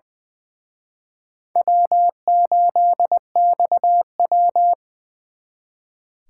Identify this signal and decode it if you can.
Morse code, heard as a keyed tone, about 24 wpm, 700 Hz. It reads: W8XW
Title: Speed Racing - USA 1x2 Call Signs 20wpm